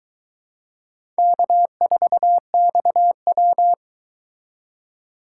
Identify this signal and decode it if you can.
Morse code, heard as a keyed tone, about 22 wpm, 700 Hz. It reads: K4XW